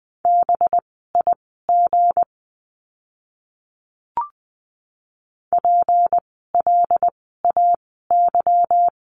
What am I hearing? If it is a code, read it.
Morse code, heard as a keyed tone, about 20 wpm, spaced as for 15 wpm, 700 Hz. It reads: BIG PLAY
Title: 237 Lesson 38f - Letter q - Words 20wpm